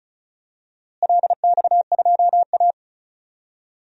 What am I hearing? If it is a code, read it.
Morse code, heard as a keyed tone, about 36 wpm, 700 Hz. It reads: LX2A
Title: DXCC-100 Call Signs - 4 Characters - Repeated 2x - 35wpm